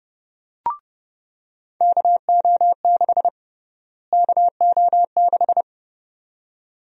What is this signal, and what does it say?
Morse code, heard as a keyed tone, about 30 wpm, 700 Hz. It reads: KO6 KO6
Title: USA Call Sign Prefixes - Repeated 2x - 30wpm